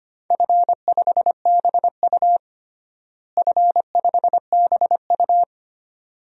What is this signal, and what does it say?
Morse code, heard as a keyed tone, about 25 wpm, 700 Hz. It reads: F5BU F5BU
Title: DXCC-100 Call Signs - 4 Characters - Repeated 2x - 25wpm